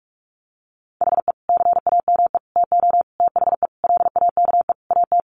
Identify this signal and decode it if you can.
Morse code, heard as a keyed tone, about 44 wpm, 700 Hz. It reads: HE CAME TO THE LAKE AT
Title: Sentences from Top 1100 Words 45wpm